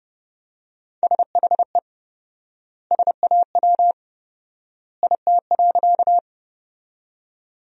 Morse code, heard as a keyed tone, about 30 wpm, 700 Hz. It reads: SHE SAW IT.
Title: 4x Word-Spacing - Sentences from Top 300 Words - 30wpm